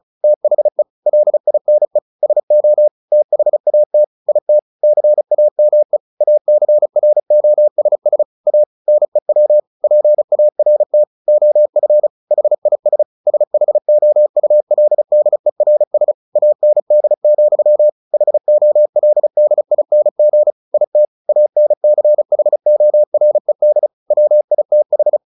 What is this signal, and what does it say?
Morse code, heard as a keyed tone, about 34 wpm, 600 Hz. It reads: THE LINE SO THAT IT CAME ACROSS A NEW PART OF HIS SHOULDERS AND, HOLDING IT ANCHORED WITH
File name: the old man and the sea / the old man and the sea 1 35wpm